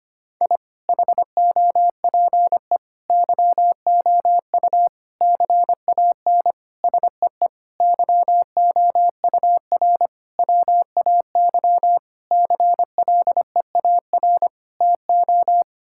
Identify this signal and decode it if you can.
Morse code, heard as a keyed tone, about 25 wpm, 700 Hz. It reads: I HOPE YOU CAN SEE YOUR WAY CLEAR TO
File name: Sets of 9 Words - 25wpm